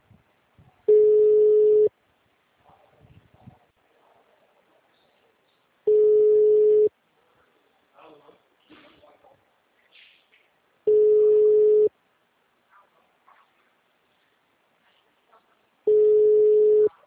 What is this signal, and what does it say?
Ringback tone (Europe, 425 Hz), with speech.